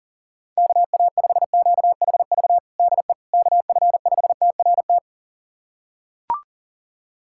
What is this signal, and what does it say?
Morse code, heard as a keyed tone, about 40 wpm, 700 Hz. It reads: KA5QHV DE KF5TRT
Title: USA Callsign DE Callsign - 6 Characters - 40wpm